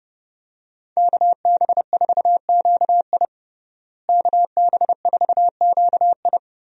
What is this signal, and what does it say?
Morse code, heard as a keyed tone, about 32 wpm, 700 Hz. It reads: KB4QI KB4QI
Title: USA Call Signs - 5 Characters - Repeated 3x - 30wpm